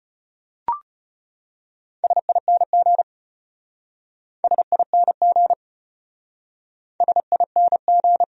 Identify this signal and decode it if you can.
Morse code, heard as a keyed tone, about 34 wpm, 700 Hz. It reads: SING SING SING